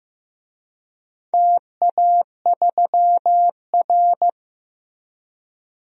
Morse code, heard as a keyed tone, about 15 wpm, 700 Hz. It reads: TA3R
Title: DXCC-100 Call Signs - 4 Characters - Repeated 3x - 15wpm